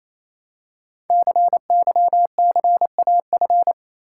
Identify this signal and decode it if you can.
Morse code, heard as a keyed tone, about 29 wpm, 700 Hz. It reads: CYCAF
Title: Code Groups - 5 Letters - 28wpm